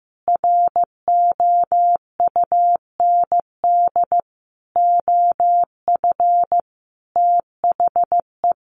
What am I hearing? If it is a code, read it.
Morse code, heard as a keyed tone, about 15 wpm, 700 Hz. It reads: ROUND OF THE
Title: Sentences from Top 700 Words 15wpm